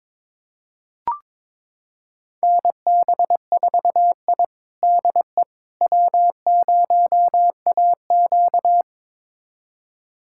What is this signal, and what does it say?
Morse code, heard as a keyed tone, about 22 wpm, 700 Hz. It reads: NB4I DE W0AQ
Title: USA Callsign DE Callsign - 4 Characters - 22wpm